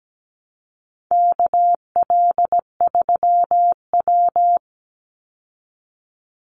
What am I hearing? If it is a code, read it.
Morse code, heard as a keyed tone, about 17 wpm, 700 Hz. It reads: KL3W
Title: Call Signs Repeated 2x - 17wpm